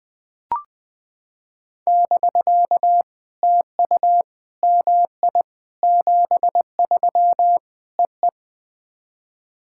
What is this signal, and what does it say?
Morse code, heard as a keyed tone, about 21 wpm, 700 Hz. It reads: <BK> TU MI 73 EE
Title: POTA QSO Practice - 20wpm